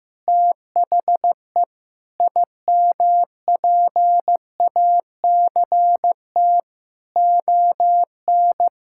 Morse code, heard as a keyed tone, about 15 wpm, 700 Hz. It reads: THE IMPACT ON
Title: Sentences from Top 1000 Words - Review Part 6 - 15wpm